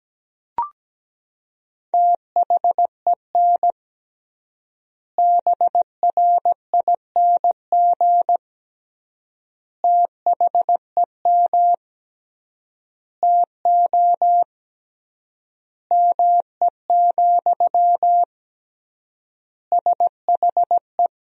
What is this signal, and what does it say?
Morse code, heard as a keyed tone, about 17 wpm, 700 Hz. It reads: THEN BRING THEM TO ME, SHE
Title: 3x Word-Spacing - Sentences from Top 500 Words - 17wpm